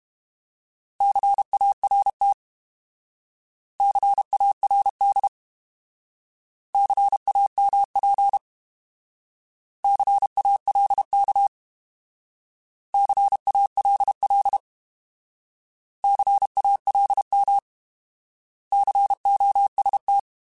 Morse code, heard as a keyed tone, about 32 wpm, 800 Hz. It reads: CART CARD CAMP CALK CALL CALM COST
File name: Alphabetical / Alphabetical mp3s / Cs @30